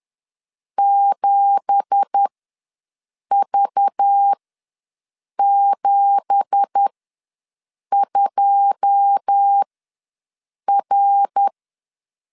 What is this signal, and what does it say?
Morse code, heard as a keyed tone, about 10 wpm, spaced as for 6 wpm, 800 Hz. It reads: 7V72R
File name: LEZIONI / 9 2867 ws 10wpm 4215